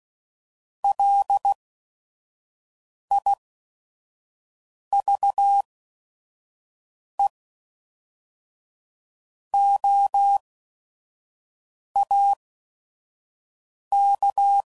Morse code, heard as a keyed tone, about 16 wpm, spaced as for 5 wpm, 800 Hz. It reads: LIVEOAK
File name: CW Code Groups @5WPM / fromSM's / Cities